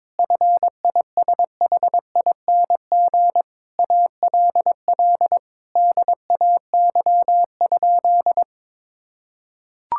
Morse code, heard as a keyed tone, about 22 wpm, 700 Hz. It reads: FISHING ALL DAY?
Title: Sentences from Top 1000 Words - Review Part 5 - 22wpm